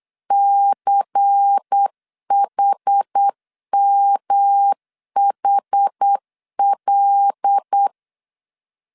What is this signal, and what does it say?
Morse code, heard as a keyed tone, about 8 wpm, 800 Hz. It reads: CHMHL
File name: LEZIONI / 6 PKX0 / 6 P 8wpm 3025